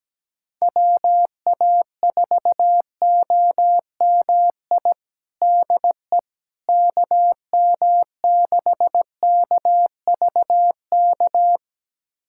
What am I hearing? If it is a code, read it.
Morse code, heard as a keyed tone, about 17 wpm, 700 Hz. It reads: WA4OMI DE KM6KVK
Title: USA Callsign DE Callsign - 6 Characters - 17wpm